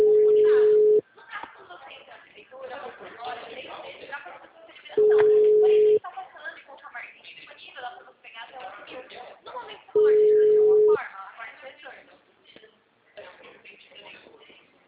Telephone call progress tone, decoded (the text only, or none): ringback tone